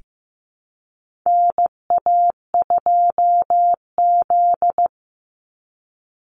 Morse code, heard as a keyed tone, about 15 wpm, 700 Hz. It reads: NA2Z
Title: USA Call Signs 15wpm